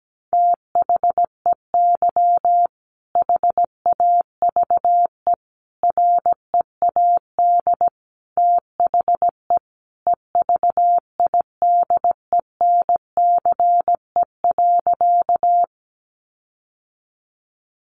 Morse code, heard as a keyed tone, about 17 wpm, 700 Hz. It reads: THEY HAVE READ THE EVIDENCE.
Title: Sentences from Top 1300 Words 17wpm